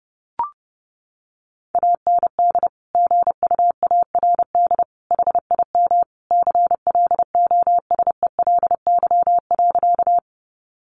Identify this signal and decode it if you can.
Morse code, heard as a keyed tone, about 30 wpm, 700 Hz. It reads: AND GUARD HIM CLOSELY.
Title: Sentences from Top 1400 Words 30wpm